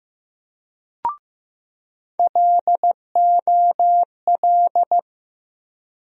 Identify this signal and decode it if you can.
Morse code, heard as a keyed tone, about 15 wpm, 700 Hz. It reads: LOL
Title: USA Call Sign Postfixes - 3 Letters 15wpm